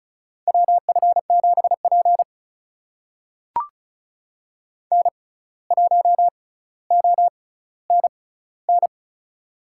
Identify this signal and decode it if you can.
Morse code, heard as a keyed tone, about 36 wpm, spaced as for 17 wpm, 700 Hz. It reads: WF7P N1ONN
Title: USA Call Signs - 6x Character Spacing - 35wpm